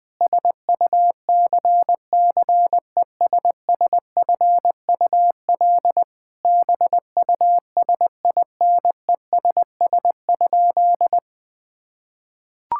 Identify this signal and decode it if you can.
Morse code, heard as a keyed tone, about 20 wpm, 700 Hz. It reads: SUCCESSFUL BUSINESS?
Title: Sentences from Top 1000 Words - Review Part 6 - 20wpm